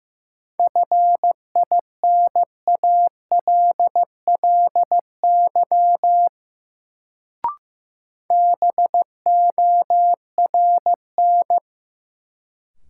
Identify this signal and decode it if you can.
Morse code, heard as a keyed tone, about 16 wpm, 700 Hz. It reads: FINALLY BORN
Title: Sentences from Top 1000 Words - New Word Drill - 15wpm